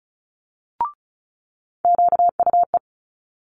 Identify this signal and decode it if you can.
Morse code, heard as a keyed tone, about 36 wpm, 700 Hz. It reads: QUE
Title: Multi-letter Phonemes Quick Fire 35wpm